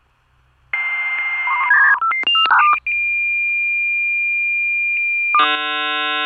Dial-up modem handshake.